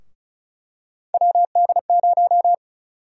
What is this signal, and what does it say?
Morse code, heard as a keyed tone, about 36 wpm, 700 Hz. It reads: WD0
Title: USA Call Sign Prefixes - Repeated 2x - 35wpm